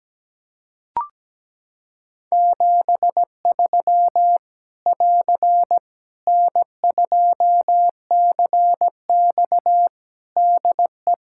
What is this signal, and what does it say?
Morse code, heard as a keyed tone, about 17 wpm, 700 Hz. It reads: 73 <AR> N2CX DE
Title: Practice QSO - Exchange 4 - Ending - 17wpm